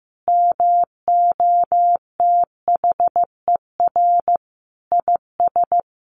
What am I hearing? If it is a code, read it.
Morse code, heard as a keyed tone, about 15 wpm, 700 Hz. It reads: MOTHER IS
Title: Sentences from Top 1100 Words 15wpm